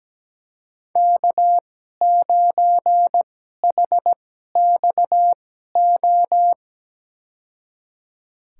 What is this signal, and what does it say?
Morse code, heard as a keyed tone, about 17 wpm, spaced as for 12 wpm, 700 Hz. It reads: K9HXO
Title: USA Call Signs - 5 Characters - 2x Character Spacing - 17wpm